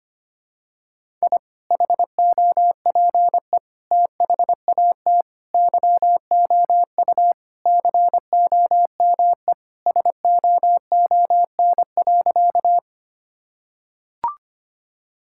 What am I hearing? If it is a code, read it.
Morse code, heard as a keyed tone, about 25 wpm, 700 Hz. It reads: I HOPE THAT YOU COME SOON.
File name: practice / 25wpm / Sentences with 2-Word Phrases - 25wpm